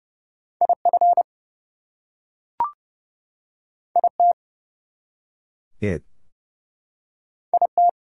Morse code, heard as a keyed tone, about 30 wpm, 700 Hz. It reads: IF IT IT
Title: Two Letter Words - 30wpm